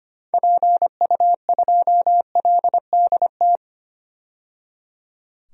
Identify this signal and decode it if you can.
Morse code, heard as a keyed tone, about 25 wpm, 690 Hz. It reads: PU2LDT
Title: DXCC-100 Call Signs - 6 Characters - 25wpm